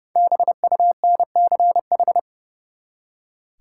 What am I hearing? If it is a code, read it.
Morse code, heard as a keyed tone, about 30 wpm, 700 Hz. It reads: BUNCH
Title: Five Letter Words - 30wpm